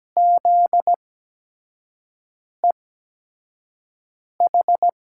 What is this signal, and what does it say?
Morse code, heard as a keyed tone, about 17 wpm, spaced as for 5 wpm, 700 Hz. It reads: ZEH
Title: USA Call Signs - 6 Characters - 8x Character Spacing - 17wpm